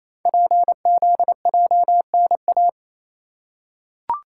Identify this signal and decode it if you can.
Morse code, heard as a keyed tone, about 27 wpm, 700 Hz. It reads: PZJNA
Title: Code Groups - 5 Letters - 28wpm